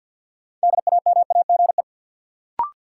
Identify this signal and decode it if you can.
Morse code, heard as a keyed tone, about 53 wpm, 700 Hz. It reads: DAMAGE
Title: Sentences from Top 1100 Words - New Word Drill Alt - 50wpm